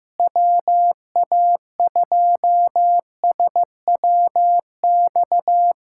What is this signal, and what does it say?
Morse code, heard as a keyed tone, about 15 wpm, 700 Hz. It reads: WA2SWX
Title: K1USN SST QSO Practice - 15wpm